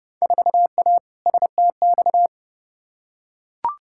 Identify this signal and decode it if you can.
Morse code, heard as a keyed tone, about 30 wpm, 700 Hz. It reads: 4A STX
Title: Field Day Exchanges - 30wpm